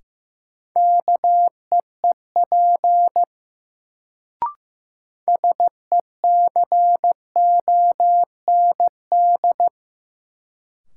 Morse code, heard as a keyed tone, about 15 wpm, 700 Hz. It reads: KEEP SECOND
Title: Sets of 2 Words - New Word Drill - 15wpm